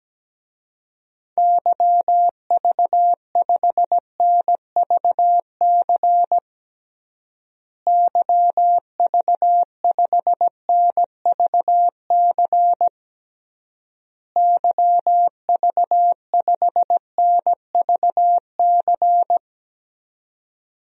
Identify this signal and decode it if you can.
Morse code, heard as a keyed tone, about 17 wpm, 700 Hz. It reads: YV5NVC YV5NVC YV5NVC
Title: DXCC-100 Call Signs - 6 Characters - Repeated 3x - 17wpm